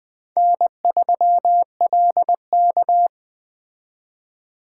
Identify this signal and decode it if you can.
Morse code, heard as a keyed tone, about 20 wpm, 700 Hz. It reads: N3LK
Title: Speed Racing - USA 1x2 Call Signs 17wpm